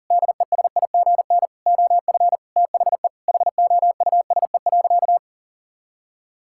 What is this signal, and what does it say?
Morse code, heard as a keyed tone, about 40 wpm, 700 Hz. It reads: DESIGN OF THE HOUSE.
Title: Sentences from Top 1000 Words - Review Part 6 - 40wpm